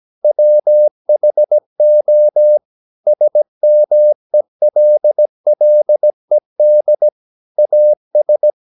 Morse code, heard as a keyed tone, about 17 wpm, 600 Hz. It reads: WHO SMELLED AS
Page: Romanen Never af Ken Follet på engelsk.